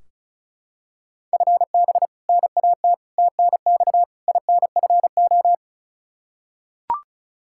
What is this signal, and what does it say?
Morse code, heard as a keyed tone, about 36 wpm, 700 Hz. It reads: FB NAT TNX INFO
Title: Practice QSO - Exchange 3 - More Info - 35wpm